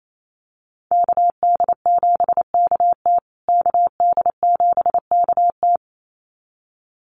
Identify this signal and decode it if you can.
Morse code, heard as a keyed tone, about 28 wpm, 700 Hz. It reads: KD7KT KD7KT
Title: USA Call Signs Repeated 2x 28wpm